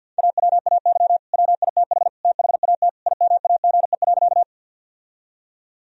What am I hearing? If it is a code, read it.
Morse code, heard as a keyed tone, about 50 wpm, 700 Hz. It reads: AWAY WITH THAT IMAGE.
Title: Sentences from Top 1000 Words - Review Part 4 - 50wpm